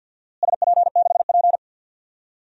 Morse code, heard as a keyed tone, about 53 wpm, 700 Hz. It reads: SP6P
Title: DXCC-100 Call Signs - 4 Characters - Repeated 2x - 50wpm